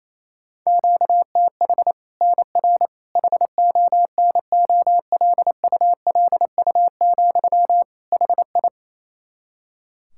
Morse code, heard as a keyed tone, about 28 wpm, 700 Hz. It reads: QTH NR HONOLULU, HI
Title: Practice QSO - Exchange 1 - Beginning - 28wpm